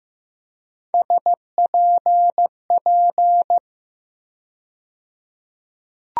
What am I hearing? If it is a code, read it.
Morse code, heard as a keyed tone, about 15 wpm, 700 Hz. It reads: SPP
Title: USA Call Sign Postfixes - 3 Letters 15wpm